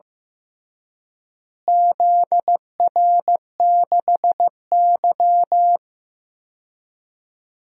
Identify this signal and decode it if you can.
Morse code, heard as a keyed tone, about 15 wpm, 700 Hz. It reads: ZR6Y
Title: DXCC-100 Call Signs - 4 Characters - Repeated 2x - 15wpm